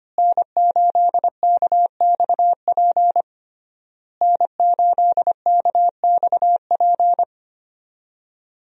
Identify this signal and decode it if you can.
Morse code, heard as a keyed tone, about 25 wpm, 700 Hz. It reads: N8KXP N8KXP
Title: USA Call Signs - 5 Characters - Repeated 3x - 25wpm